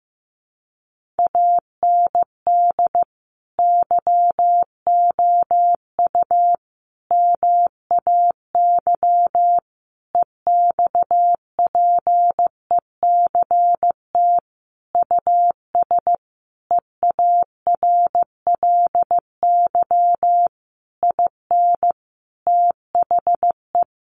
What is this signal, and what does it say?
Morse code, heard as a keyed tone, about 15 wpm, 700 Hz. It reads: AND YOU MAY EXPECT US EARLY IN THE
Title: Sentences from Top 1500 Words 15wpm